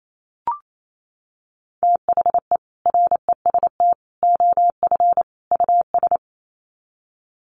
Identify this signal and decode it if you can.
Morse code, heard as a keyed tone, about 28 wpm, 700 Hz. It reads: THE REST OF US
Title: Sets of 4 Words 28wpm